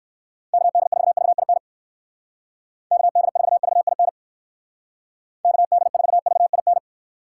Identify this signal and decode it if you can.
Morse code, heard as a keyed tone, about 59 wpm, 700 Hz. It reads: KD4VIN KD4VIN KD4VIN